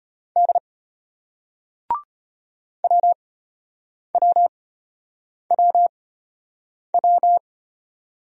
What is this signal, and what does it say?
Morse code, heard as a keyed tone, about 32 wpm, spaced as for 8 wpm, 700 Hz. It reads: D WWWW